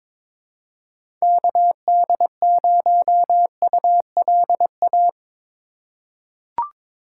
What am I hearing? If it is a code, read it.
Morse code, heard as a keyed tone, about 22 wpm, 700 Hz. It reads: KD0ULA
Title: USA Call Signs - 6 Characters - 4x Character Spacing - 22wpm